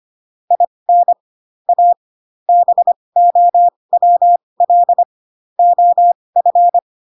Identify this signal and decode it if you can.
Morse code, heard as a keyed tone, about 25 wpm, spaced as for 20 wpm, 700 Hz. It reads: IN A BOWL OF
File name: Pinocchio / Pinocchio ch8 15 25 ik1hge